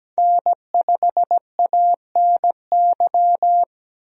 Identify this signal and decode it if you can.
Morse code, heard as a keyed tone, about 17 wpm, 700 Hz. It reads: N5ANY
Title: USA Call Signs - 5 Characters - 17wpm